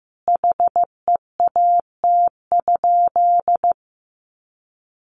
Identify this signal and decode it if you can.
Morse code, heard as a keyed tone, about 15 wpm, 700 Hz. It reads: HEAT?
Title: Sentences from Top 1400 Words 15wpm